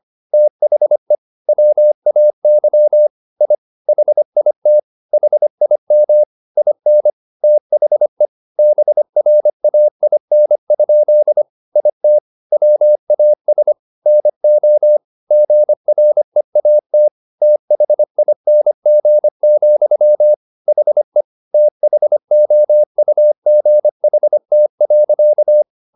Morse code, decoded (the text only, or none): THE WAY I HIT HIM IN THE BRAIN? IT WAS NO GREAT THING, HE THOUGHT.